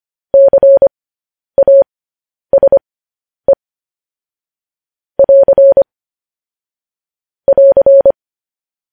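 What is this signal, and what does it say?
Morse code, heard as a keyed tone, about 25 wpm, spaced as for 10 wpm, 550 Hz. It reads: CASE <AR> <AR>
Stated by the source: Morse code news headlines broadcast.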